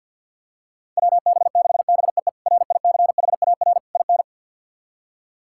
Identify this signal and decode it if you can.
Morse code, heard as a keyed tone, about 53 wpm, 700 Hz. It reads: WB6BEE RICHAR IN
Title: K1USN SST QSO Practice - 50wpm